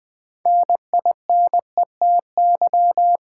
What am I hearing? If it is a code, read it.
Morse code, heard as a keyed tone, about 20 wpm, 700 Hz. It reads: NINETY
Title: Build A Word - 3 Letters At A Time - 6-Letter Words - 20wpm